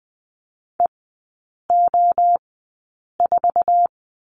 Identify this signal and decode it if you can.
Morse code, heard as a keyed tone, about 20 wpm, spaced as for 8 wpm, 700 Hz. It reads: EO4
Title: 047 Lesson 9e - Number 4 - Three Characters 20wpm